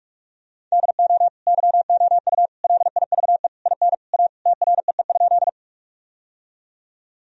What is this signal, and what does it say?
Morse code, heard as a keyed tone, about 43 wpm, 700 Hz. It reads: DO YOU LIVE IN A TREE?
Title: Sentences from Top 1000 Words - Review Part 1 - 45wpm